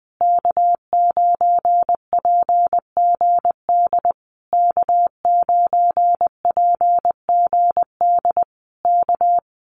Morse code, heard as a keyed tone, about 19 wpm, 700 Hz. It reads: K9PGD K9PGD K
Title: Common QSO Phrases Encore 20wpm